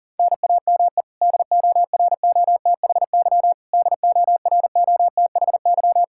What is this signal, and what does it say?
Morse code, heard as a keyed tone, about 40 wpm, 700 Hz. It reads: NAME DOROTHY DOROTHY
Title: Practice QSO - Exchange 1 - Beginning - 40wpm